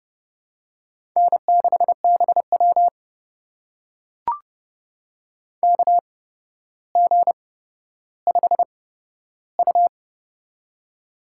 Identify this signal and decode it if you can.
Morse code, heard as a keyed tone, about 30 wpm, spaced as for 13 wpm, 700 Hz. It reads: N6BW KG5U
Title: USA Call Signs - 4 Characters - 8x Character Spacing - 30wpm